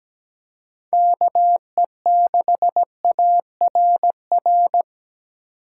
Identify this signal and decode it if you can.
Morse code, heard as a keyed tone, about 17 wpm, 700 Hz. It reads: KE6ARR
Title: USA Call Signs - 6 Characters - 3x Character Spacing - 17wpm